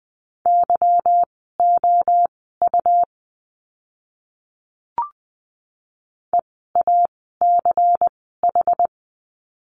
Morse code, heard as a keyed tone, about 20 wpm, spaced as for 15 wpm, 700 Hz. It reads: YOU EACH
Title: 192 Lesson 31f - Letter v - Words 20wpm